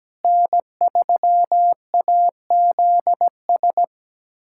Morse code, heard as a keyed tone, about 17 wpm, 700 Hz. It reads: N3AZS
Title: USA Callsign DE Callsign - 5 Characters - 17wpm